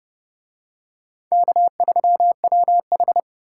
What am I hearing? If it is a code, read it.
Morse code, heard as a keyed tone, about 28 wpm, 700 Hz. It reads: K3WH